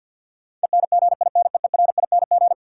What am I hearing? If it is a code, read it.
Morse code, heard as a keyed tone, about 51 wpm, 700 Hz. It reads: ENGINEERING
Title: Sentences from Top 1500 Words - New Word Drill Alt - 50wpm